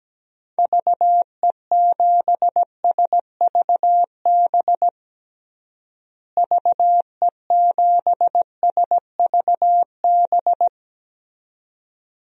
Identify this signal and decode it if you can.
Morse code, heard as a keyed tone, about 17 wpm, 700 Hz. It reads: VE7SVB VE7SVB
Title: DXCC-100 Call Signs - 6 Characters - Repeated 2x - 17wpm